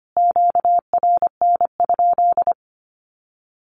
Morse code, heard as a keyed tone, about 25 wpm, 700 Hz. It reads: QRN?
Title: Q-Codes Repeated 3x 25wpm